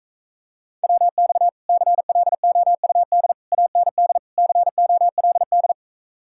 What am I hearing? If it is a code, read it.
Morse code, heard as a keyed tone, about 45 wpm, 700 Hz. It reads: WX CLOUD AND COLD